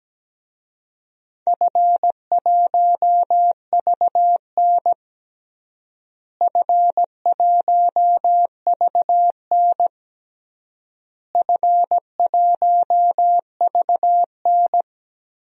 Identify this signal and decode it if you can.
Morse code, heard as a keyed tone, about 17 wpm, 700 Hz. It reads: F1VN F1VN F1VN
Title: DXCC-100 Call Signs - 4 Characters - Repeated 3x - 17wpm